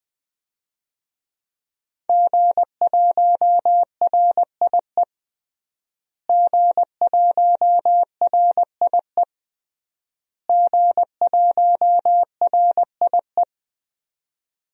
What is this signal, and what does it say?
Morse code, heard as a keyed tone, about 20 wpm, 700 Hz. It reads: G1RIE G1RIE G1RIE
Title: DXCC-100 Call Signs - 5 Characters - Repeated 3x - 20wpm